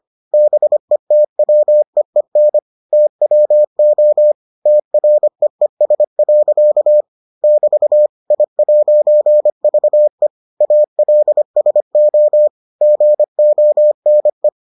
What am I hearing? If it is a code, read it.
Morse code, heard as a keyed tone, about 25 wpm, 600 Hz. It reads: BETWEEN TWO TREES. <BT> I'VE ALSO GONE